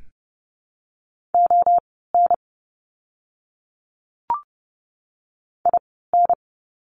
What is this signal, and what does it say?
Morse code, heard as a keyed tone, about 29 wpm, spaced as for 17 wpm, 700 Hz. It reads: ON IN
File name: course / 30:10wpm / 230 Lesson 37f - Letter x - Words 30wpm